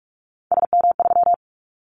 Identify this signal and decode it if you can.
Morse code, heard as a keyed tone, about 46 wpm, 700 Hz. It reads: SM3
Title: Callsign Prefixes for Top 25 DXCC - 45wpm